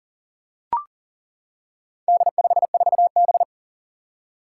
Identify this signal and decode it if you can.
Morse code, heard as a keyed tone, about 40 wpm, 700 Hz. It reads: D54B
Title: Four Challenging Letter-Number Combinations 40wpm